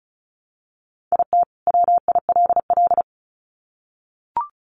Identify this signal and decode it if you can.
Morse code, heard as a keyed tone, about 34 wpm, 700 Hz. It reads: IT WILL
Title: Sets of 2 Words 35wpm